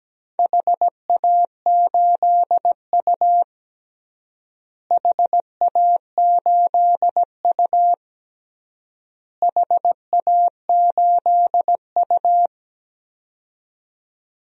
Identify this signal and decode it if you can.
Morse code, heard as a keyed tone, about 17 wpm, 700 Hz. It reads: HA8U HA8U HA8U
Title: DXCC-100 Call Signs - 4 Characters - Repeated 3x - 17wpm